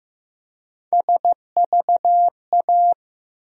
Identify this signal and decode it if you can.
Morse code, heard as a keyed tone, about 15 wpm, 700 Hz. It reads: SVA
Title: USA Call Sign Postfixes - 3 Letters 15wpm